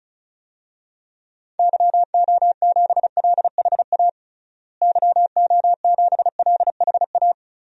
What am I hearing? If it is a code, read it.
Morse code, heard as a keyed tone, about 34 wpm, 700 Hz. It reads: YO7LHA YO7LHA
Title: DXCC-100 Call Signs - 6 Characters - Repeated 3x - 35wpm